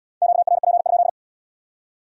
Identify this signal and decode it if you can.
Morse code, heard as a keyed tone, about 71 wpm, 700 Hz. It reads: QRP?